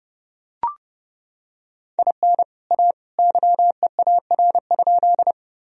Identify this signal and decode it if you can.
Morse code, heard as a keyed tone, about 32 wpm, 700 Hz. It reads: IN A YEAR?
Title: Sentences from Top 1000 Words - Review Part 1 - 30wpm